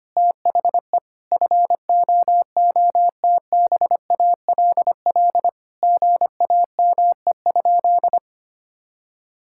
Morse code, heard as a keyed tone, about 25 wpm, 700 Hz. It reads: THE FOOTBALL GAME?
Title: Sentences from Top 1600 Words 25wpm